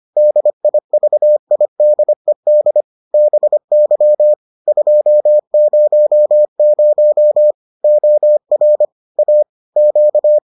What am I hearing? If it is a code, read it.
Morse code, heard as a keyed tone, about 25 wpm, 600 Hz. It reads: DIVIDED BY 200 OR A Q